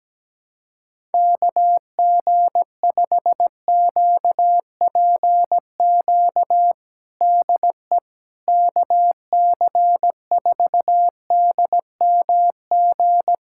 Morse code, heard as a keyed tone, about 17 wpm, 700 Hz. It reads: KG5QPQ DE KC4DMG
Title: USA Callsign DE Callsign - 6 Characters - 17wpm